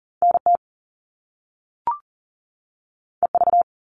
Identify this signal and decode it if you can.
Morse code, heard as a keyed tone, about 40 wpm, 700 Hz. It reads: NT EV
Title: All Cut Numbers - 2-Digit 40wpm